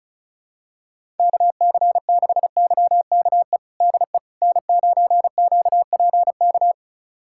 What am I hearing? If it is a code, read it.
Morse code, heard as a keyed tone, about 36 wpm, 700 Hz. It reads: KC6YKE DE N9QPK
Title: USA Callsign DE Callsign 35wpm